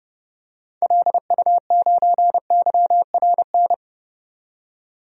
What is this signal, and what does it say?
Morse code, heard as a keyed tone, about 30 wpm, 700 Hz. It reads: LU9YRN
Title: DXCC-100 Call Signs - 6 Characters - 30wpm